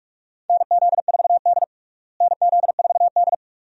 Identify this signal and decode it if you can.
Morse code, heard as a keyed tone, about 43 wpm, 700 Hz. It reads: NZ4D NZ4D
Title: USA Call Signs - 4 Characters - Repeated 2x - 45wpm